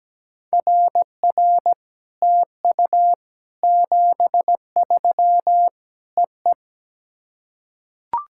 Morse code, heard as a keyed tone, about 17 wpm, 700 Hz. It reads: RR TU 73 EE
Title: SOTA QSO Practice 17wpm